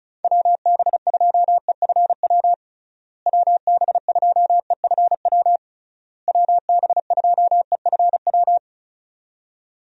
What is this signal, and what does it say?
Morse code, heard as a keyed tone, about 36 wpm, 700 Hz. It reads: WB2EFW WB2EFW WB2EFW
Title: USA Call Signs - 6 Characters - Repeated 3x - 35wpm